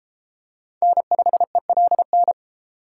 Morse code, heard as a keyed tone, about 31 wpm, 700 Hz. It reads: N5ELN